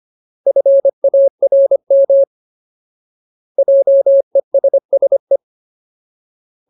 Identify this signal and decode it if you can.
Morse code, heard as a keyed tone, about 25 wpm, 550 Hz. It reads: FARM JESSE
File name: Ultimate Word List / 25WPM Ultimate